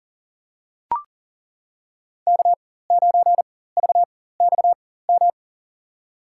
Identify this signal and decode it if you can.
Morse code, heard as a keyed tone, about 40 wpm, spaced as for 19 wpm, 700 Hz. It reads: K9VXM
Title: USA Call Signs - 5 Characters - 4x Character Spacing - 40wpm